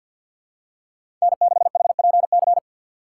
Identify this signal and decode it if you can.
Morse code, heard as a keyed tone, about 48 wpm, 690 Hz. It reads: N6HPC
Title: USA Call Signs - 5 Characters - 4x Character Spacing - 50wpm